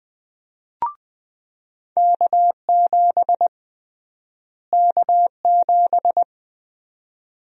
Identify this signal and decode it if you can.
Morse code, heard as a keyed tone, about 21 wpm, 700 Hz. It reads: K7 K7
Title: USA Call Sign Prefixes - Repeated 2x - 20wpm